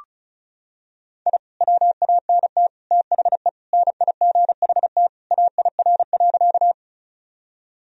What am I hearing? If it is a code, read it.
Morse code, heard as a keyed tone, about 36 wpm, 700 Hz. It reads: I WANT THE NIGHT AIR.
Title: Sentences from Top 1000 Words - Review Part 1 - 35wpm